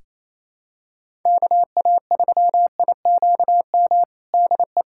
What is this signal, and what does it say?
Morse code, heard as a keyed tone, about 28 wpm, 700 Hz. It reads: KA3IQM DE
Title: USA Callsign DE Callsign - 6 Characters - 28wpm